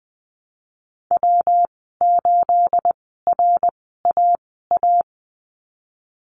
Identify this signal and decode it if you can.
Morse code, heard as a keyed tone, about 20 wpm, spaced as for 15 wpm, 700 Hz. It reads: W8RAA
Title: 219 Lesson 35g - Number 0 - Call Signs 20wpm